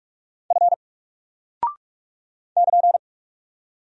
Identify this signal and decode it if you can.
Morse code, heard as a keyed tone, about 44 wpm, 700 Hz. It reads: F <KN>
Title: Single Character (Letter:Number:Punctuation:Prosign) Alt 45wpm